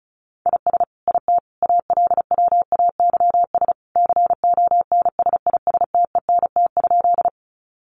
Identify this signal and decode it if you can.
Morse code, heard as a keyed tone, about 34 wpm, 700 Hz. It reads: IS IT ALWAYS CONSISTENT?
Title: Sentences from Top 1500 Words 35wpm